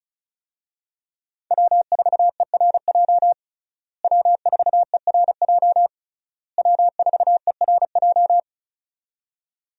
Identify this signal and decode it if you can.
Morse code, heard as a keyed tone, about 34 wpm, 700 Hz. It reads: W4ERJ W4ERJ W4ERJ
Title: USA Call Signs - 5 Characters - Repeated 3x - 35wpm